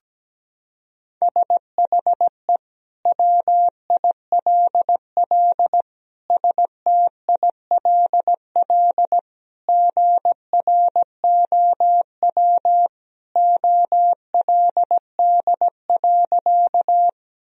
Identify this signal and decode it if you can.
Morse code, heard as a keyed tone, about 17 wpm, 700 Hz. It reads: SHE WILL STILL GROW OLD.
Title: Sentences from Top 1000 Words - Review Part 1 - 17wpm